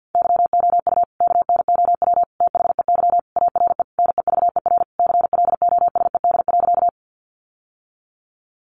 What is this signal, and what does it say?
Morse code, heard as a keyed tone, about 50 wpm, 700 Hz. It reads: YOU KNOW THEY ARE NEVER CLOSED.
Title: Sentences from Top 1400 Words 50wpm